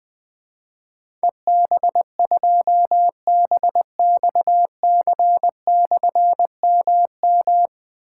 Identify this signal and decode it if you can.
Morse code, heard as a keyed tone, about 20 wpm, 700 Hz. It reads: EB2BXC/MM
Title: Difficult Call Signs 20wpm